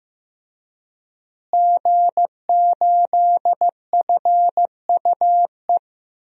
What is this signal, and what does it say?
Morse code, heard as a keyed tone, about 15 wpm, 700 Hz. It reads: G8FUE
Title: DXCC-100 Call Signs - 5 Characters - Repeated 3x - 15wpm